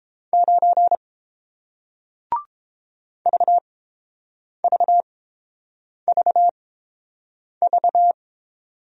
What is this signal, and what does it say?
Morse code, heard as a keyed tone, about 30 wpm, spaced as for 7 wpm, 700 Hz. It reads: 9 VVVV